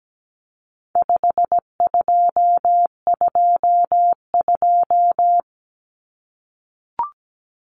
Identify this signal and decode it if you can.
Morse code, heard as a keyed tone, about 17 wpm, 700 Hz. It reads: 5222
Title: Numbers - 4-Digit 17wpm